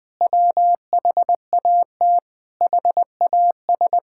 Morse code, heard as a keyed tone, about 20 wpm, 700 Hz. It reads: WHAT HAS
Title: Sentences from Top 1000 Words - Review Part 5 - 20wpm